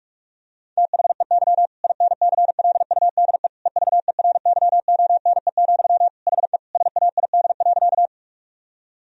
Morse code, heard as a keyed tone, about 46 wpm, 700 Hz. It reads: THEY INCLUDE EVERYONE, HE SAID.
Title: Sentences from Top 1000 Words - Review Part 3 - 45wpm